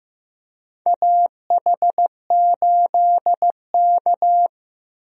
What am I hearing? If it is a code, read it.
Morse code, heard as a keyed tone, about 15 wpm, 700 Hz. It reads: AH8K
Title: USA Call Signs - 4 Characters - Repeated 3x - 15wpm